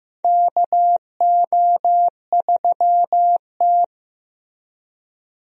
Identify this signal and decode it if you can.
Morse code, heard as a keyed tone, about 15 wpm, 700 Hz. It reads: KO3T
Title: USA Call Signs - 4 Characters - Repeated 2x - 15wpm